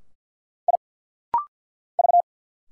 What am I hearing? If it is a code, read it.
Morse code, heard as a keyed tone, about 49 wpm, 700 Hz. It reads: I V
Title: Single Letters - Rapid-Fire 50wpm